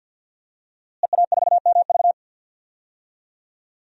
Morse code, heard as a keyed tone, about 52 wpm, 700 Hz. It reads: EA4MV
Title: DXCC-100 Call Signs - 5 Characters - Repeated 3x - 50wpm